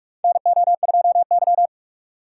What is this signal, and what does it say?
Morse code, heard as a keyed tone, about 46 wpm, 700 Hz. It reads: NO2Y
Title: USA Call Signs - 4 Characters - 4x Character Spacing - 45wpm